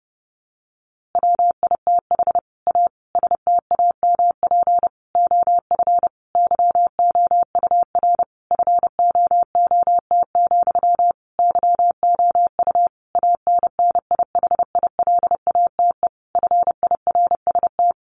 Morse code, heard as a keyed tone, about 28 wpm, 700 Hz. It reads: WITH A STAMP OF YOUR FOOT, YOU ANNIHILATE FIRST
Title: A Sound Of Thunder 30wpm